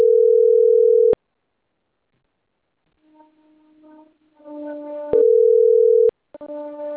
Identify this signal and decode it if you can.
Ringback tone (North America), 440 + 480 Hz.